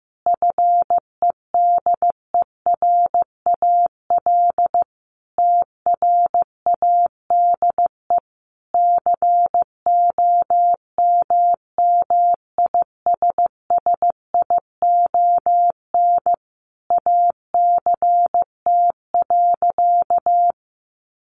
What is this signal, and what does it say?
Morse code, heard as a keyed tone, about 15 wpm, 700 Hz. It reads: FEDERAL TRADE COMMISSION ACT.
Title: Sentences from Top 1400 Words 15wpm